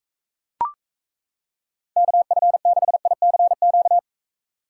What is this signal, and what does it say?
Morse code, heard as a keyed tone, about 35 wpm, 700 Hz. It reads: KF6ICQ